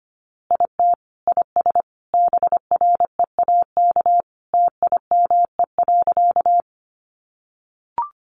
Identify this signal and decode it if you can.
Morse code, heard as a keyed tone, about 25 wpm, 700 Hz. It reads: IT IS BREAK TIME.
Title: Easy Phrases 25wpm